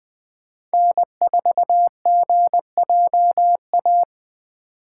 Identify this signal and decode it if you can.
Morse code, heard as a keyed tone, about 20 wpm, 700 Hz. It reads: N4GJA
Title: USA Call Signs - 5 Characters - 2x Character Spacing - 20wpm